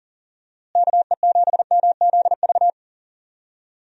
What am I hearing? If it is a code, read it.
Morse code, heard as a keyed tone, about 39 wpm, 700 Hz. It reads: KE7MZV
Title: USA Call Signs - 6 Characters - Repeated 2x - 40wpm